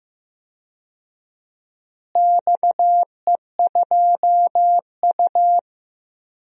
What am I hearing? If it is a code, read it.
Morse code, heard as a keyed tone, about 15 wpm, 700 Hz. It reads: XE2U
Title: DXCC-100 Call Signs - 4 Characters - Repeated 3x - 15wpm